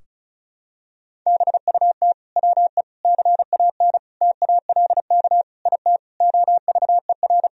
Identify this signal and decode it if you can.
Morse code, heard as a keyed tone, about 36 wpm, 700 Hz. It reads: BUT WE CAN TALK IT OVER
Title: Sentences from Top 1000 Words - Review Part 4 - 35wpm